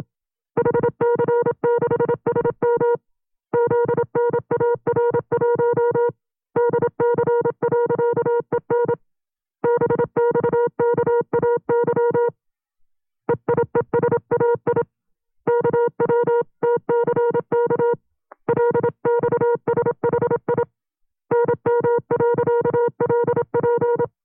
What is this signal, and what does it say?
Morse code, heard as a keyed tone, about 27 wpm, 480 Hz. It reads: HC6SM ZNAR1 DC.EN BXKAY EIESAI KWTCK LXSHI NM.LP
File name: CW-practice/25WPM